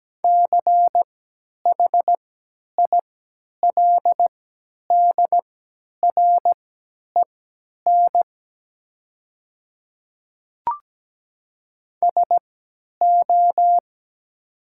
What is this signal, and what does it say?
Morse code, heard as a keyed tone, about 17 wpm, spaced as for 10 wpm, 700 Hz. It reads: CHILDREN SO
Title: 3x Character Spacing - Top 500 Words - 17wpm